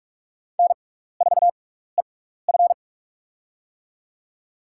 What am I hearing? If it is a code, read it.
Morse code, heard as a keyed tone, about 47 wpm, spaced as for 16 wpm, 700 Hz. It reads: N4EF
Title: USA Call Signs - 4 Characters - 6x Character Spacing - 45wpm